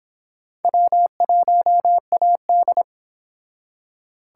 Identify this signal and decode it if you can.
Morse code, heard as a keyed tone, about 25 wpm, 700 Hz. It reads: W1AD